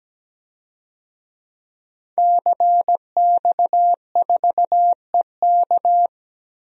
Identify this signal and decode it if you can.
Morse code, heard as a keyed tone, about 17 wpm, 700 Hz. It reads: CX4EK
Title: DXCC-100 Call Signs - 5 Characters - 17wpm